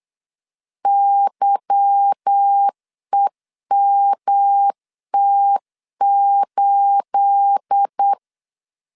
Morse code, heard as a keyed tone, about 8 wpm, 800 Hz. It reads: YEMT8
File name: LEZIONI / 9 28 8wpm 3512